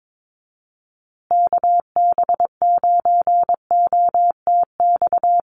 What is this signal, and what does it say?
Morse code, heard as a keyed tone, about 22 wpm, 700 Hz. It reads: KB9OTX
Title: USA Call Signs Repeated 2x 22wpm